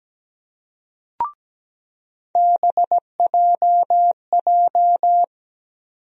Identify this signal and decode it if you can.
Morse code, heard as a keyed tone, about 17 wpm, 700 Hz. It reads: BJJ
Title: USA Call Sign Postfixes - 3 Letters 17wpm